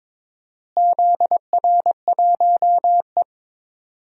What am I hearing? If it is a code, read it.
Morse code, heard as a keyed tone, about 21 wpm, 700 Hz. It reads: ZR1E
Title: DXCC-100 Call Signs - 4 Characters - Repeated 2x - 22wpm